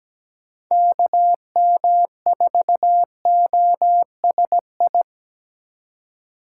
Morse code, heard as a keyed tone, about 17 wpm, 700 Hz. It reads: KM4OSI
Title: USA Call Signs - 6x Character Spacing - 17wpm